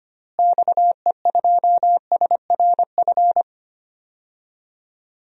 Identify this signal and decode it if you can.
Morse code, heard as a keyed tone, about 25 wpm, 700 Hz. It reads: XE2SRF
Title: DXCC-100 Call Signs - 6 Characters - Repeated 3x - 25wpm